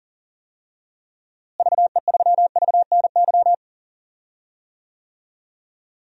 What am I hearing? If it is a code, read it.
Morse code, heard as a keyed tone, about 40 wpm, 700 Hz. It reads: VE3VNY
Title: DXCC-100 Call Signs - 6 Characters - Repeated 3x - 40wpm